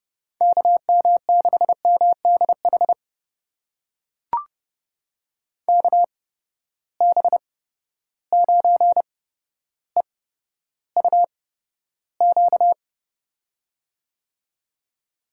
Morse code, heard as a keyed tone, about 31 wpm, spaced as for 13 wpm, 700 Hz. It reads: KM6MDH KB9EUQ
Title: USA Call Signs - 8x Character Spacing - 30wpm